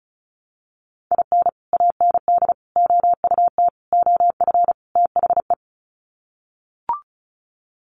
Morse code, heard as a keyed tone, about 34 wpm, 700 Hz. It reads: IN AND OUT OF THE
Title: Sets of 5 Words 35wpm